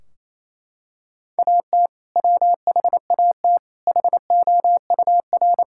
Morse code, heard as a keyed tone, about 28 wpm, 700 Hz. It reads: AT WHAT HOUR
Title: Sentences from Top 1600 Words 28wpm